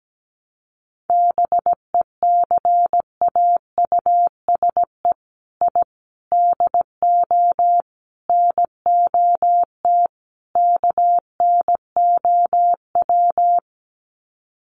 Morse code, heard as a keyed tone, about 17 wpm, 700 Hz. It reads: BECAUSE I DO NOT KNOW
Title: Sets of 5 Words Encore2 17wpm